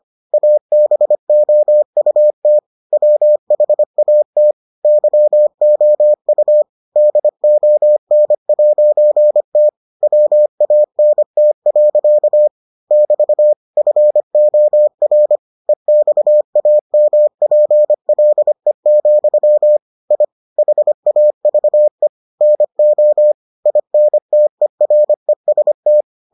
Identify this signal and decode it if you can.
Morse code, heard as a keyed tone, about 25 wpm, 600 Hz. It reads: ABOUT WHAT YOU DON'T WANT. <BT> FOR EXAMPLE, I HAVE NO INTEREST